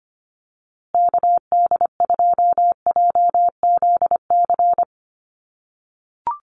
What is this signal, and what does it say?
Morse code, heard as a keyed tone, about 25 wpm, 700 Hz. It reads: KD2JZC
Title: USA Call Signs Repeated 2x 25wpm